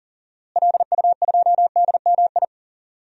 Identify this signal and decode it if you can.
Morse code, heard as a keyed tone, about 40 wpm, 700 Hz. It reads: LU2DMI
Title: DXCC-100 Call Signs - 6 Characters - Repeated 3x - 40wpm